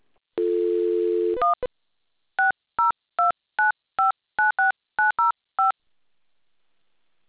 DTMF, heard as telephone touch tones, tones of about 140 ms. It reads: 1 6029596905